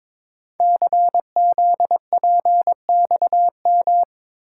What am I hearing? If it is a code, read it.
Morse code, heard as a keyed tone, about 23 wpm, 700 Hz. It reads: CZPXM
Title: Code Groups - 5 Letters - 22wpm